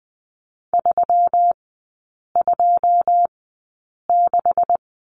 Morse code, heard as a keyed tone, about 20 wpm, spaced as for 8 wpm, 700 Hz. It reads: 326
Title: 127 Lesson 21e - Number 6 - Three Characters 20wpm